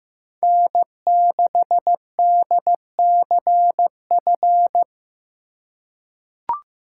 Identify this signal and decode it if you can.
Morse code, heard as a keyed tone, about 15 wpm, 700 Hz. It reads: N6DCF
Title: USA Call Signs - 5 Characters - 2x Character Spacing - 15wpm